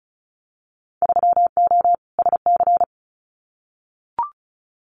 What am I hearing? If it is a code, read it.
Morse code, heard as a keyed tone, about 36 wpm, 700 Hz. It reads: 3O SC
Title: Winter Field Day Exchanges Encore 35wpm